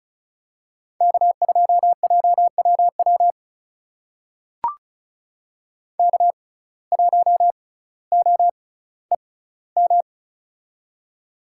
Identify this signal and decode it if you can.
Morse code, heard as a keyed tone, about 35 wpm, spaced as for 18 wpm, 700 Hz. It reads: K2JWW K1OEM
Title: USA Call Signs - 5 Characters - 6x Character Spacing - 35wpm